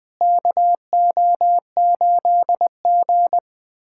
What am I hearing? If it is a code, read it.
Morse code, heard as a keyed tone, about 21 wpm, 700 Hz. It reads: KO8G
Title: USA Callsign DE Callsign - 4 Characters - 20wpm